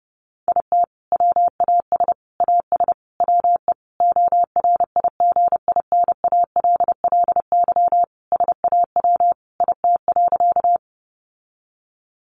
Morse code, heard as a keyed tone, about 30 wpm, 700 Hz. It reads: IT WAS AS WE ORIGINALLY SAW IT.
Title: Sentences from Top 1500 Words 30wpm